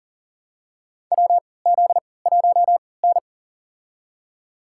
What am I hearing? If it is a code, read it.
Morse code, heard as a keyed tone, about 40 wpm, spaced as for 23 wpm, 700 Hz. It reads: WZ1N
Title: USA Call Signs - 3x Character Spacing - 40wpm